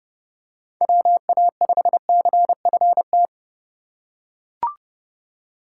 Morse code, heard as a keyed tone, about 31 wpm, 700 Hz. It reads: WA5CFT